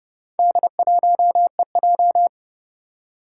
Morse code, heard as a keyed tone, about 30 wpm, 700 Hz. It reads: D1EJ
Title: Four Challenging Letter-Number Combinations 30wpm